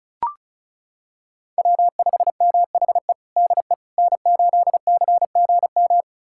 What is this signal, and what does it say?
Morse code, heard as a keyed tone, about 34 wpm, 700 Hz. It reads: W5MHE DE N8CGM
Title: USA Callsign DE Callsign - 5 Characters - 35wpm